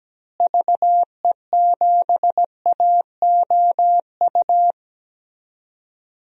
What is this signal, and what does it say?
Morse code, heard as a keyed tone, about 17 wpm, 700 Hz. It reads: VE7AOU
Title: DXCC-100 Call Signs - 6 Characters - Repeated 2x - 17wpm